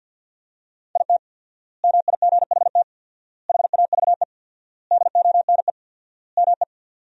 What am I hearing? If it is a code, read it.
Morse code, heard as a keyed tone, about 47 wpm, 700 Hz. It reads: IT MIGHT HAVE DONE ME
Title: 4x Word-Spacing - Sentences from Top 500 Words - 50wpm